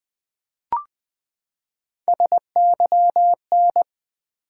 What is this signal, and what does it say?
Morse code, heard as a keyed tone, about 20 wpm, 700 Hz. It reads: SYN
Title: Build A Word - 3 Letters At A Time - 9-Letter Words - 20wpm